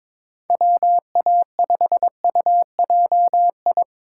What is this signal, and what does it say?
Morse code, heard as a keyed tone, about 21 wpm, 700 Hz. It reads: WA5UJI
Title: USA Call Signs - 6 Characters - 2x Character Spacing - 22wpm